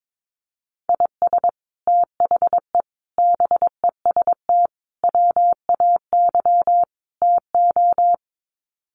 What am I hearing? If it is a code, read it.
Morse code, heard as a keyed tone, about 22 wpm, 700 Hz. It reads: IS THE BEST WAY TO
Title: Sets of 5 Words Encore 22wpm